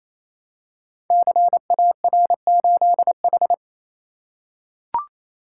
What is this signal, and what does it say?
Morse code, heard as a keyed tone, about 29 wpm, 690 Hz. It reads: CAR8H
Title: Code Groups - 5 Letter:Numbers - 28wpm